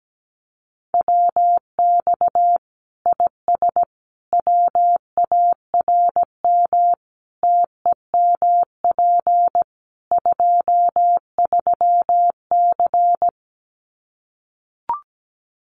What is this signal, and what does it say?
Morse code, heard as a keyed tone, about 17 wpm, 700 Hz. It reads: WX IS WARM TEMP 23C
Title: Common QSO Phrases 17wpm